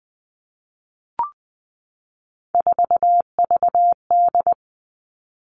Morse code, heard as a keyed tone, about 20 wpm, 700 Hz. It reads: 4VD
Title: Three Challenging Letter-Number Combinations 20wpm